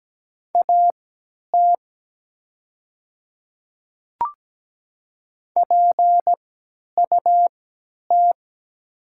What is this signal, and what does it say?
Morse code, heard as a keyed tone, about 17 wpm, spaced as for 10 wpm, 700 Hz. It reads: AT PUT
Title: 3x Character Spacing - Top 300 Words - 17wpm